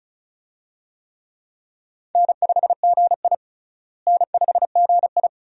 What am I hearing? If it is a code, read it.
Morse code, heard as a keyed tone, about 34 wpm, 700 Hz. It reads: N5GI N5GI
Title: USA Call Signs - 4 Characters - Repeated 2x - 35wpm